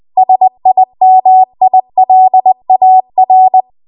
Morse code, heard as a keyed tone, about 20 wpm, 750 Hz. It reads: SIMILAR